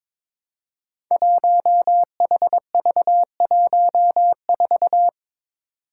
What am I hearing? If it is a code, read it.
Morse code, heard as a keyed tone, about 23 wpm, 700 Hz. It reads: 1HV14
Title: Five Challenging Letter-Number Combinations 22wpm